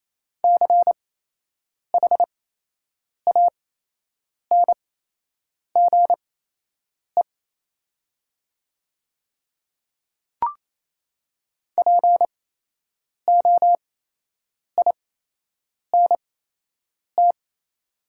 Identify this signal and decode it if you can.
Morse code, heard as a keyed tone, about 28 wpm, spaced as for 8 wpm, 700 Hz. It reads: CHANGE POINT
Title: 8x Character Spacing - Top 200 Words - 28wpm